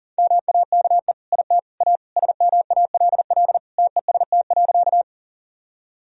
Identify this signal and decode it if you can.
Morse code, heard as a keyed tone, about 37 wpm, 700 Hz. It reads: MAKE IT A SMALL TEST.
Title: Sentences from Top 1000 Words - Review Part 3 - 40wpm